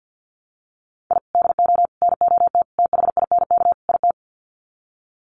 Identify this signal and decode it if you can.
Morse code, heard as a keyed tone, about 50 wpm, 700 Hz. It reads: I DO NOT THINK IT
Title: Sets of 5 Words 50wpm